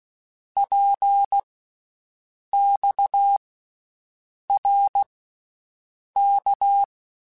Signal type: Morse code